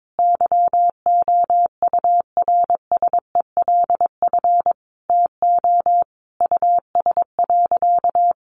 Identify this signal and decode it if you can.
Morse code, heard as a keyed tone, about 22 wpm, 700 Hz. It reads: YOURSELF TO US.
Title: Sentences from Top 600 Words 22wpm